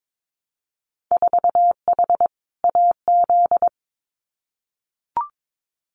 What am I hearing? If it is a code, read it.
Morse code, heard as a keyed tone, about 22 wpm, 700 Hz. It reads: 4H AZ
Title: Winter Field Day Exchanges Encore 22wpm